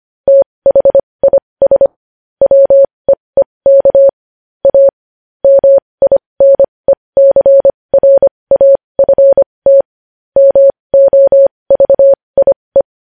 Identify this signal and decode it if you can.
Morse code, heard as a keyed tone, about 25 wpm, spaced as for 20 wpm, 550 Hz. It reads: THIS WEEK A MINECRAFT MOVIE
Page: news headlines in morse code at 20 wpm